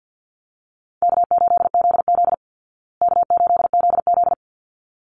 Morse code, heard as a keyed tone, about 49 wpm, 700 Hz. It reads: K8ZZ K8ZZ
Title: Easy Call Signs Repeated 2x - 50wpm